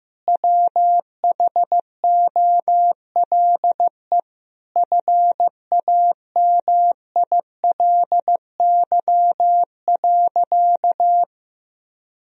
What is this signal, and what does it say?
Morse code, heard as a keyed tone, about 15 wpm, 700 Hz. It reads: WHOLE FAMILY.
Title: Sentences from Top 1000 Words - Review Part 4 - 15wpm